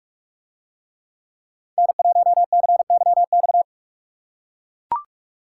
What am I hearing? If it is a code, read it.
Morse code, heard as a keyed tone, about 43 wpm, 700 Hz. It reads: N1CYX